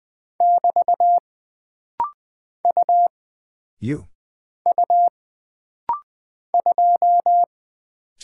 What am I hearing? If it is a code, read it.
Morse code, heard as a keyed tone, about 20 wpm, 700 Hz. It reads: <BT> U U 2
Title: Single Character (Letter:Number:Punctuation:Prosign) - Rapid-Fire - 20wpm